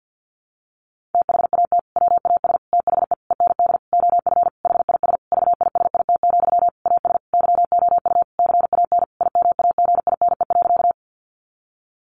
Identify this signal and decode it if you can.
Morse code, heard as a keyed tone, about 50 wpm, 700 Hz. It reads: THAT WAS THE END OF HIS VISIT, AS YOU CAN IMAGINE.
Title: Sentences from Top 1400 Words 50wpm